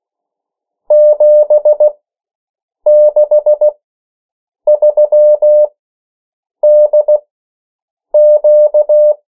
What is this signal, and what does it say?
Morse code, heard as a keyed tone, about 16 wpm, spaced as for 7 wpm, 600 Hz. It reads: 763DQ
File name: morse mp3 / 14 urok 13